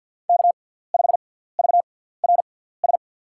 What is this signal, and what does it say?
Morse code, heard as a keyed tone, about 53 wpm, spaced as for 17 wpm, 700 Hz. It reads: K5VRS
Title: USA Call Signs - 5 Characters - 6x Character Spacing - 50wpm